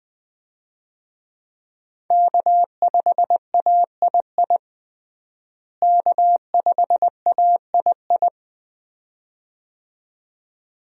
Morse code, heard as a keyed tone, about 21 wpm, 700 Hz. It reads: K5AII K5AII
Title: USA Call Signs - 5 Characters - Repeated 2x - 20wpm